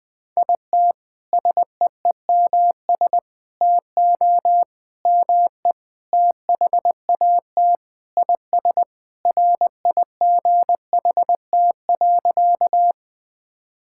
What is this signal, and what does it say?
Morse code, heard as a keyed tone, about 20 wpm, 690 Hz. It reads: IT SEEMS TO ME THAT IS RIGHT.
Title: Sentences with 4-Word Phrases - Spoken Beginning - 20wpm